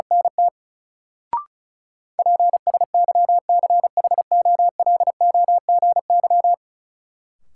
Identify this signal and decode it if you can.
Morse code, heard as a keyed tone, about 36 wpm, 700 Hz. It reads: NT PSYCHOLOGY
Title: Sentences from Top 1400 Words - New Word Drill Alt - 35wpm